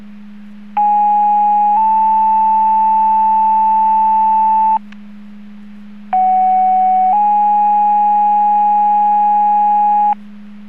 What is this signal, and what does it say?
Two-tone page, heard as a one-second tone, then a three-second tone, about 830 then 880 Hz, about 750 then 830 Hz. It is heard from 0.8 to 10.1 s.
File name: Portage 154.385 Paging